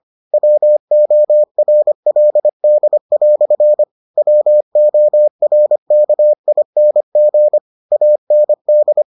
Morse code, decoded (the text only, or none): WORLD" WORKING AND